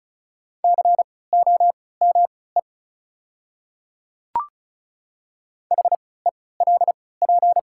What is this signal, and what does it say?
Morse code, heard as a keyed tone, about 36 wpm, spaced as for 20 wpm, 690 Hz. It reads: COME HELP
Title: 3x Character Spacing - Top 300 Words - 35wpm